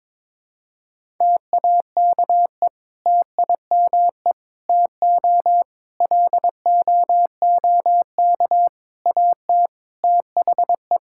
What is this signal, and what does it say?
Morse code, heard as a keyed tone, about 22 wpm, 700 Hz. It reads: TAKE TIME TO LOOK AT THE
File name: practice / 22wpm / Sentences with 4-Word Phrases - 22wpm